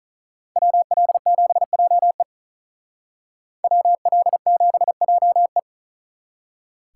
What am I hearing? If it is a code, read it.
Morse code, heard as a keyed tone, about 45 wpm, 700 Hz. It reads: WL7JE WL7JE